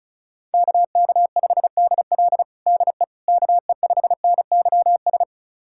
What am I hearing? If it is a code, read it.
Morse code, heard as a keyed tone, about 36 wpm, 700 Hz. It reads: KK5DL DE KE5NYS
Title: USA Callsign DE Callsign 35wpm